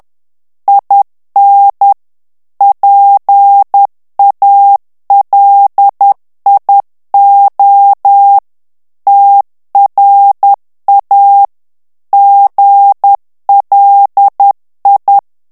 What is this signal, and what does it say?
Morse code, heard as a keyed tone, about 10 wpm, 800 Hz. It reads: IN PALIO TRA GLI
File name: ITALIEN texte clair 2 vitesse 10 mots minute